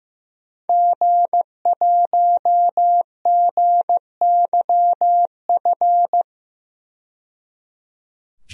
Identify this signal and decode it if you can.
Morse code, heard as a keyed tone, about 15 wpm, 700 Hz. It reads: G1GYF
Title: DXCC-100 Call Signs - 5 Characters - 15wpm